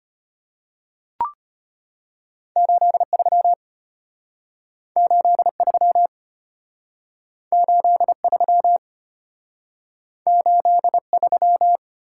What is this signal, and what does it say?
Morse code, heard as a keyed tone, about 33 wpm, 700 Hz. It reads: 83 83 83 83